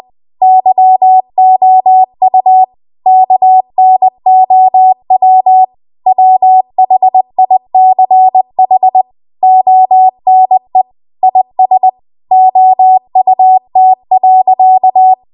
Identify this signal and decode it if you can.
Morse code, heard as a keyed tone, about 20 wpm, 750 Hz. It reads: YOU KNOW WHICH ONE IS OUT.
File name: Morse / 20WPM